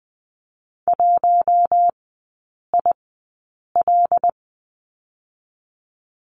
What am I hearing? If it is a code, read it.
Morse code, heard as a keyed tone, about 20 wpm, 700 Hz. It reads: 1 I L
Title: 114 Lesson 19e - Letter w - Three Characters 20wpm